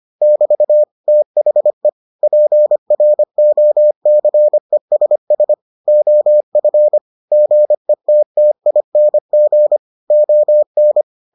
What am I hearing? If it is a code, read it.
Morse code, heard as a keyed tone, about 25 wpm, 600 Hz. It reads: <BT> THE PROCESS OF GETTING ON